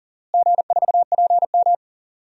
Morse code, heard as a keyed tone, about 40 wpm, 700 Hz. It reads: G4PM
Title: DXCC-100 Call Signs - 4 Characters - Repeated 3x - 40wpm